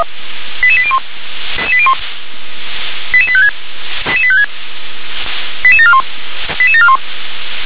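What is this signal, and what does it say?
Five-tone selective call (ZVEI), heard as quick runs of five tones, tones of about 70 ms. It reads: 88801 88856 88856 88741 88741
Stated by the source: ZVEI five-tone selective call received by SDR.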